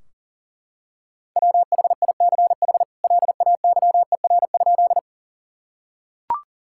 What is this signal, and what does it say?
Morse code, heard as a keyed tone, about 37 wpm, 700 Hz. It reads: WHICH LAYER?
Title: Sentences from Top 1600 Words 40wpm